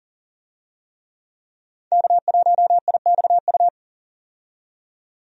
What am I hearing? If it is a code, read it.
Morse code, heard as a keyed tone, about 37 wpm, 700 Hz. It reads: K1IXU
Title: USA Call Signs - 5 Characters - 40wpm